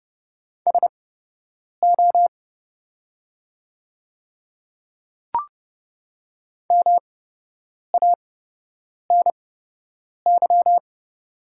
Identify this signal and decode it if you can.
Morse code, heard as a keyed tone, about 30 wpm, spaced as for 8 wpm, 700 Hz. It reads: SO MANY
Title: 8x Character Spacing - Top 100 Words - 30wpm